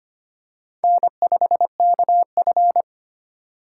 Morse code, heard as a keyed tone, about 25 wpm, 700 Hz. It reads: N5KF
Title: USA Call Signs - 4 Characters - 3x Character Spacing - 25wpm